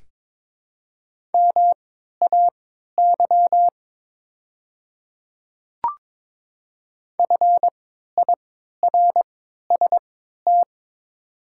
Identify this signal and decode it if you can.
Morse code, heard as a keyed tone, about 21 wpm, spaced as for 12 wpm, 700 Hz. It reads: MAY FIRST
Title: 3x Character Spacing - Top 100 Words - 22wpm